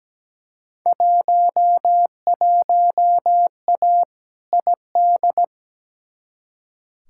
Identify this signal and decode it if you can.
Morse code, heard as a keyed tone, about 17 wpm, 700 Hz. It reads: 11A ID
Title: Field Day Exchanges Encore - 17wpm